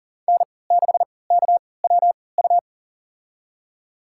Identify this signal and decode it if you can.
Morse code, heard as a keyed tone, about 39 wpm, spaced as for 22 wpm, 700 Hz. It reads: N6KWU
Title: USA Call Signs - 5 Characters - 3x Character Spacing - 40wpm